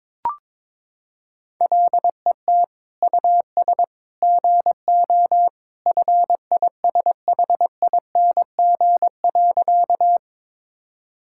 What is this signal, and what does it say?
Morse code, heard as a keyed tone, about 22 wpm, 700 Hz. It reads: LET US GO FISHING.
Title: Sentences from Top 1000 Words - Review Part 5 - 22wpm